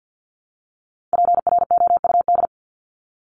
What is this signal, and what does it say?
Morse code, heard as a keyed tone, about 50 wpm, 700 Hz. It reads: PROUD
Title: Top 1501-1750 Words 50wpm